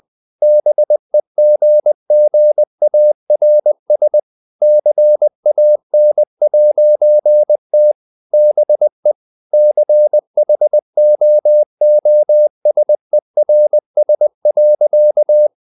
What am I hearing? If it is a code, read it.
Morse code, heard as a keyed tone, about 20 wpm, 600 Hz. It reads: BEGGARS CAN'T BE CHOOSERS.
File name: CW / Practice / EnglishSayings20